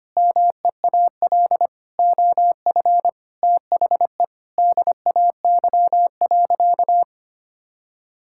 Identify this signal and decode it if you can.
Morse code, heard as a keyed tone, about 25 wpm, 700 Hz. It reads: MEAL OF THE DAY.
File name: practice / 25wpm / Difficult Sentences with Hints - 25wpm